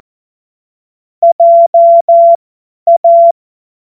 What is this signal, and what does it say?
Morse code, heard as a keyed tone, about 14 wpm, spaced as for 10 wpm, 680 Hz. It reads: JA
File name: Morsekurs / Witze / Witz 26 März 2024 50 F70